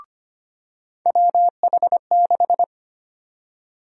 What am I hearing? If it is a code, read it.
Morse code, heard as a keyed tone, about 25 wpm, 700 Hz. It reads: WH6
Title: USA Call Sign Prefixes - 25wpm